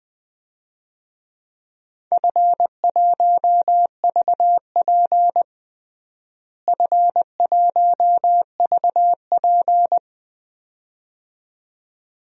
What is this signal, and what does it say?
Morse code, heard as a keyed tone, about 20 wpm, 700 Hz. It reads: F1VP F1VP
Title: DXCC-100 Call Signs - 4 Characters - Repeated 2x - 20wpm